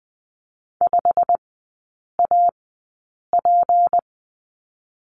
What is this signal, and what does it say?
Morse code, heard as a keyed tone, about 20 wpm, spaced as for 8 wpm, 700 Hz. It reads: 5AP
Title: 198 Lesson 32e - Letter k - Three Characters 20wpm